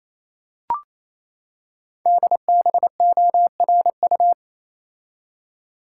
Morse code, heard as a keyed tone, about 27 wpm, 700 Hz. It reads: DBORU
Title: Code Groups - 5 Letters - 28wpm